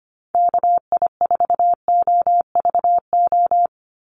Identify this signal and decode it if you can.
Morse code, heard as a keyed tone, about 25 wpm, 700 Hz. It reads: KI4OVO
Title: USA Call Signs 25wpm